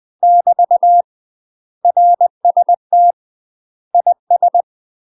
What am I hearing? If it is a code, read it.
Morse code, heard as a keyed tone, about 20 wpm, 700 Hz. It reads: <BT> RST IS
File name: UK MASTER QSOs TRAINING / 20WPM / UK MASTER QSOs 0004